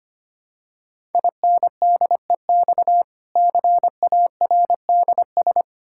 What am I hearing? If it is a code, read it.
Morse code, heard as a keyed tone, about 25 wpm, 700 Hz. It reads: INDEX CARDS
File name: practice / 25wpm / Difficult Words with Hints - 25wpm